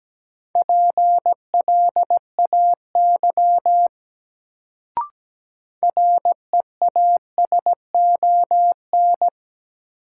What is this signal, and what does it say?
Morse code, heard as a keyed tone, about 17 wpm, 700 Hz. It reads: PLAY REASON
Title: Sets of 5 Words Encore - New Word Drill - 17wpm